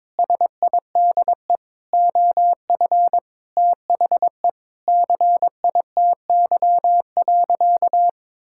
Morse code, heard as a keyed tone, about 22 wpm, 700 Hz. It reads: SIDE OF THE CITY.
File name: practice / 22wpm / Sentences with 4-Word Phrases - 22wpm